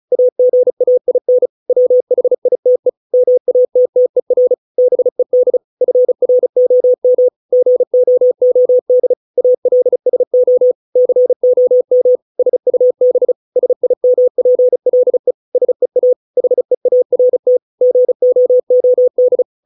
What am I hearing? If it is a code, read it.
Morse code, heard as a keyed tone, about 34 wpm, 500 Hz. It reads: AGAIN WHITE MATTER BED FROM GOOD ALSO COM SUB SIMPLE SEA HEART GOOD